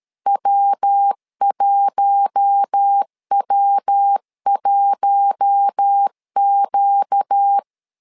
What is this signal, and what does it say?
Morse code, heard as a keyed tone, about 13 wpm, 800 Hz. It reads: W1W1Q
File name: LEZIONI / 4 Q1 12wpm 2028